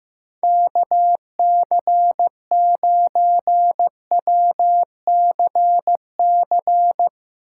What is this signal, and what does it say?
Morse code, heard as a keyed tone, about 17 wpm, 700 Hz. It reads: KC9WCC
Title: Speed Racing - USA 2x3 Call Signs 15wpm